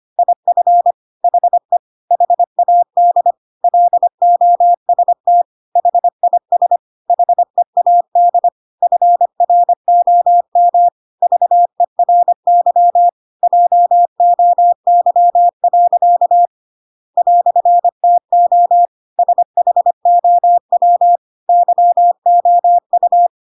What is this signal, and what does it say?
Morse code, heard as a keyed tone, about 25 wpm, 700 Hz. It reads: IF HE HAD LOST HIS HEAD FROM VERY JOY. "TO SHOW YOU